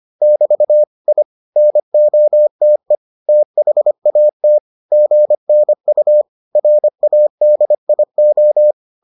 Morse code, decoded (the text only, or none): <BT> I NOTE THAT GNU RADIO